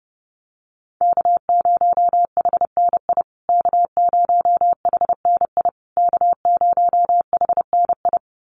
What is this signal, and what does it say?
Morse code, heard as a keyed tone, about 30 wpm, 700 Hz. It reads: K0HNI K0HNI K0HNI
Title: USA Call Signs Repeated 3x 30wpm